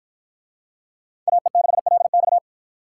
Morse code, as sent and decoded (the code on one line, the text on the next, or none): .- . -.... .-.. -..-
AE6LX